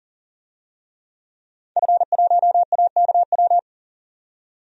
Morse code, heard as a keyed tone, about 40 wpm, 700 Hz. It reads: F1AKW
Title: DXCC-100 Call Signs - 5 Characters - 40wpm